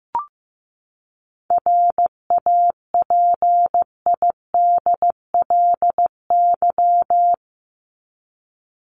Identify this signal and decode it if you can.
Morse code, heard as a keyed tone, about 15 wpm, 700 Hz. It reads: RAPIDLY
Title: LY Suffix - Top 100 Words with Base Words 15wpm